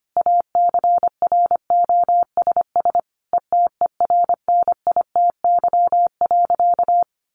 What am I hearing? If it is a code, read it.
Morse code, heard as a keyed tone, about 25 wpm, 700 Hz. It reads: ACROSS ETERNITY.
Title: A Sound Of Thunder 25wpm